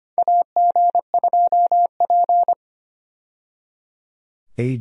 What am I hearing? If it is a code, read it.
Morse code, heard as a keyed tone, about 25 wpm, 700 Hz. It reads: AG2P
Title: USA Call Signs - 4 Characters - 25wpm